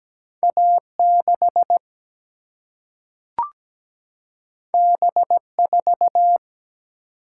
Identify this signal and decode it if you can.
Morse code, heard as a keyed tone, about 17 wpm, 700 Hz. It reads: A 6 B 4
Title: Single-Letter Single-Number 17wpm